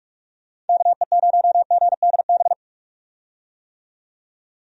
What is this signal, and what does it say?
Morse code, heard as a keyed tone, about 41 wpm, 700 Hz. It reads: KE0GDB
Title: USA Call Signs - 6 Characters - 45wpm